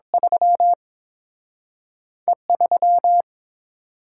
Morse code, heard as a keyed tone, about 30 wpm, 700 Hz. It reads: 3 E 3